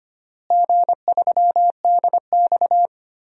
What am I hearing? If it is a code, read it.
Morse code, heard as a keyed tone, about 25 wpm, 700 Hz. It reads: G3DX
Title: DXCC-100 Call Signs - 4 Characters - Repeated 2x - 25wpm